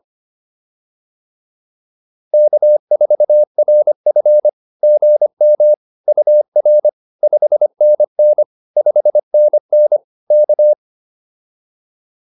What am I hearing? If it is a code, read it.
Morse code, heard as a keyed tone, about 25 wpm, 600 Hz. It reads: K4RF GM UR 5NN 5NN K
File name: POTA102 25